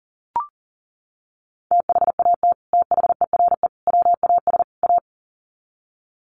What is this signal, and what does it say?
Morse code, heard as a keyed tone, about 40 wpm, 700 Hz. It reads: THAT THERE WAS A
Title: Sets of 4 Words 40wpm